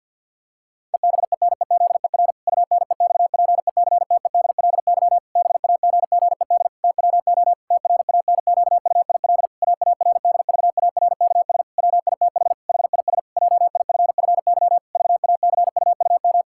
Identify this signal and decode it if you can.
Morse code, heard as a keyed tone, about 50 wpm, 700 Hz. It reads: EBENEZER UNEXPECTEDLY BAGGED TWO TRANQUIL AARDVARKS WITH HIS JIFFY VACUUM
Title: Pangram Challenge v2 - 50wpm